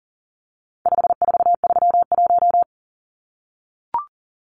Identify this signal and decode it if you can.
Morse code, heard as a keyed tone, about 40 wpm, 700 Hz. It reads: 5431
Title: Numbers - 4-Digit 40wpm